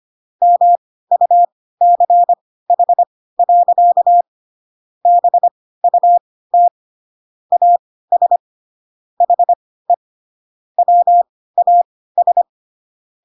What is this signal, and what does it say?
Morse code, heard as a keyed tone, about 25 wpm, spaced as for 16 wpm, 700 Hz. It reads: MUCH. BUT AS HE WAS
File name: Pinocchio / Pinocchio ch2 10 25 ik1hge